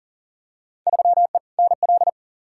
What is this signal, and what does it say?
Morse code, heard as a keyed tone, about 40 wpm, 700 Hz. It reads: 3E NL
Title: Field Day Exchanges Encore - 40wpm